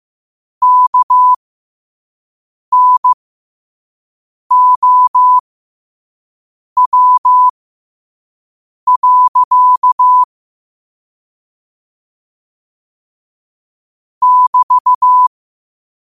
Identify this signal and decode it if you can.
Morse code, heard as a keyed tone, about 15 wpm, spaced as for 5 wpm, 1000 Hz. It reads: KNOW. <BT>
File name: qotdfiles / 5wpm